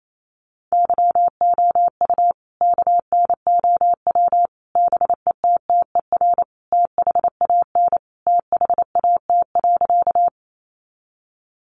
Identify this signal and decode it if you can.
Morse code, heard as a keyed tone, about 28 wpm, 700 Hz. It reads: YOU KNOW BETTER THAN THAT.
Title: Sentences from Top 400 Words 28wpm